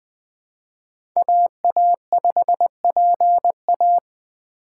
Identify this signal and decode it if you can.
Morse code, heard as a keyed tone, about 21 wpm, 700 Hz. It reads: AA5PA
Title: Speed Racing - USA 2x2 Call Signs 17wpm